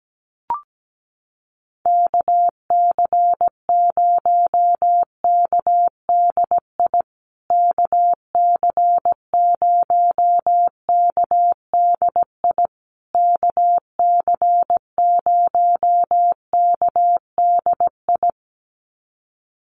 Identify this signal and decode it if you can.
Morse code, heard as a keyed tone, about 17 wpm, 700 Hz. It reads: KC0KDI KC0KDI KC0KDI
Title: USA Call Signs Repeated 3x 17wpm